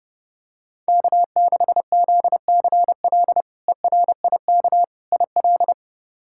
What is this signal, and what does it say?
Morse code, heard as a keyed tone, about 30 wpm, 700 Hz. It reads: K6ZCL ERIK IL
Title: K1USN SST QSO Practice - 30wpm